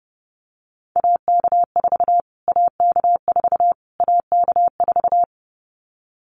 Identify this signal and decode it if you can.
Morse code, heard as a keyed tone, about 30 wpm, 700 Hz. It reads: AK4 AK4 AK4
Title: Callsign Prefixes for Top 25 DXCC - Repeated 3x - 30wpm